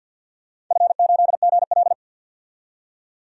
Morse code, heard as a keyed tone, about 47 wpm, 700 Hz. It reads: F8GL
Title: DXCC-100 Call Signs - 4 Characters - 50wpm